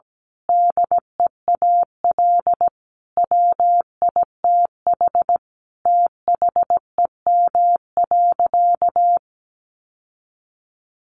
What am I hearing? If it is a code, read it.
Morse code, heard as a keyed tone, about 17 wpm, 700 Hz. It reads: DEAL WITH THEM.
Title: Sentences from Top 800 Words 17wpm